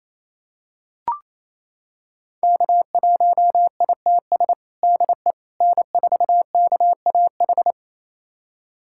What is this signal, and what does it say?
Morse code, heard as a keyed tone, about 28 wpm, 700 Hz. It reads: K1ITS DE N4KAH
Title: USA Callsign DE Callsign - 5 Characters - 28wpm